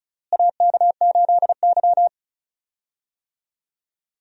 Morse code, heard as a keyed tone, about 34 wpm, 700 Hz. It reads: AK8Y
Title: USA Call Signs - 4 Characters - Repeated 2x - 35wpm